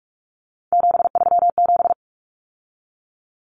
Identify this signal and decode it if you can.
Morse code, heard as a keyed tone, about 44 wpm, 700 Hz. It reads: 737
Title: Numbers - 3-Digit 45wpm